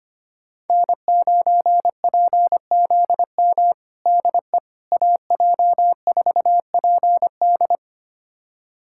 Morse code, heard as a keyed tone, about 25 wpm, 700 Hz. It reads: N9PZM DE AJ4PD
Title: USA Callsign DE Callsign - 5 Characters - 25wpm